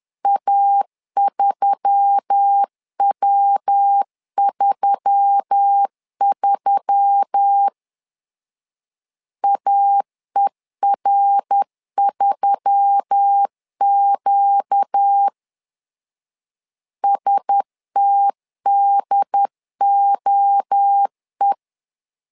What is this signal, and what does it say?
Morse code, heard as a keyed tone, about 10 wpm, 800 Hz. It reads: A3W33 AER3Q STDOE
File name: LEZIONI / 4 Q1S3 10wpm 2427